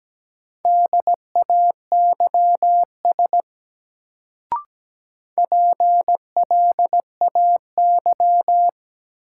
Morse code, heard as a keyed tone, about 17 wpm, 700 Hz. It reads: DAYS PLAY
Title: Sets of 5 Words Encore - New Word Drill - 17wpm